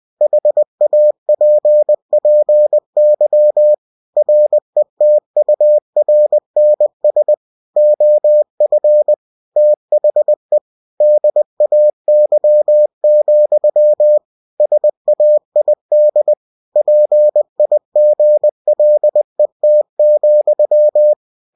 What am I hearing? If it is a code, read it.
Morse code, heard as a keyed tone, about 20 wpm, 600 Hz. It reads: HAPPY RETURNS OF THE DAY, SAID PIGLET,
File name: cw-books / winnie-the-pooh / winnie-20wpm-page-0054